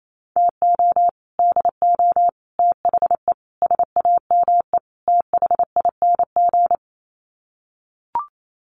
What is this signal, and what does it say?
Morse code, heard as a keyed tone, about 28 wpm, 700 Hz. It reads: TO DO THE SAME THING
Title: Sets of 5 Words Encore 28wpm